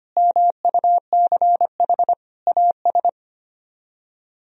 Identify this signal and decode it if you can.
Morse code, heard as a keyed tone, about 32 wpm, 700 Hz. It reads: MUCH AS